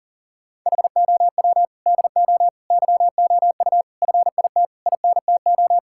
Morse code, heard as a keyed tone, about 40 wpm, 690 Hz. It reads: HOW DO YOU FIT INTO
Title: Sentences from Top 1000 Words - Review Part 6 - 40wpm